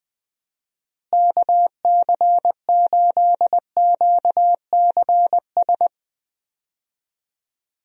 Morse code, heard as a keyed tone, about 20 wpm, 700 Hz. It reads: KC8QCS
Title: USA Call Signs - 6 Characters - Repeated 3x - 20wpm